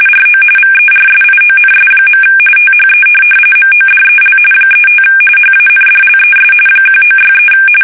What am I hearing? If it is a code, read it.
RTTY, about 45.45 baud. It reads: RO VERSIONS. THE MODEL 35 HANDLES 1963
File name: fsk850 / teletype-corp